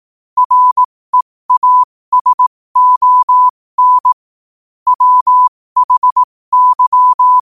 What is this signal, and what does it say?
Morse code, heard as a keyed tone, about 18 wpm, spaced as for 15 wpm, 1000 Hz. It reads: REASON WHY
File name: qotdfiles / 15wpm